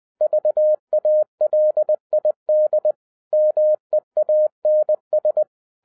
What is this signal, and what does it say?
Morse code, heard as a keyed tone, about 20 wpm, 610 Hz. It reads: VALID MEANS